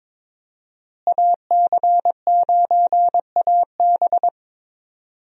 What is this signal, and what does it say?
Morse code, heard as a keyed tone, about 22 wpm, 700 Hz. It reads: AC9AB
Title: USA Call Signs - 5 Characters - Repeated 2x - 22wpm